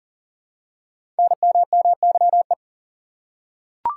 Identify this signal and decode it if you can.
Morse code, heard as a keyed tone, about 39 wpm, 700 Hz. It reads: NMMYE
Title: Code Groups - 5 Letters - 40wpm